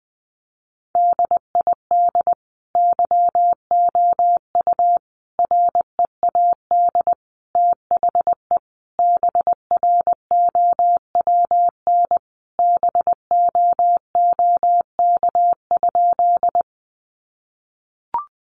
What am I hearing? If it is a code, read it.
Morse code, heard as a keyed tone, about 20 wpm, 700 Hz. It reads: DID YOU READ THE BROWN BOOK?
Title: Sentences from Top 1200 Words 20wpm